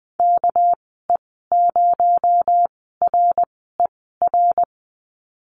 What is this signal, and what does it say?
Morse code, heard as a keyed tone, about 20 wpm, spaced as for 14 wpm, 700 Hz. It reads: KE0RER
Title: 238 Lesson 38g - Letter q - Call Signs 20wpm